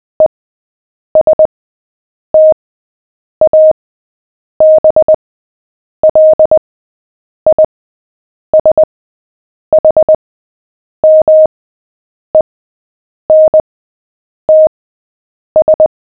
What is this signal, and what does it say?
Morse code, heard as a keyed tone, about 20 wpm, spaced as for 8 wpm, 620 Hz. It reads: ESTABLISHMENTS
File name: cw / fists / Short-Story-7-8wpm